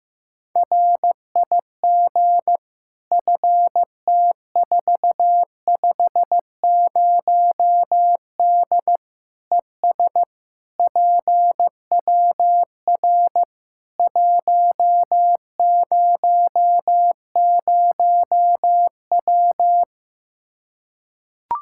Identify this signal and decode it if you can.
Morse code, heard as a keyed tone, about 15 wpm, 700 Hz. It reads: RIG FT450D ES PWR 100W
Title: Practice QSO - Exchange 2 - Info - 15wpm